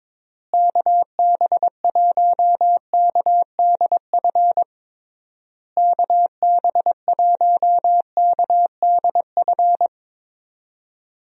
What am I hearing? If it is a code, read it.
Morse code, heard as a keyed tone, about 22 wpm, 700 Hz. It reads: KB1KDF KB1KDF
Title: USA Call Signs - 6 Characters - Repeated 3x - 22wpm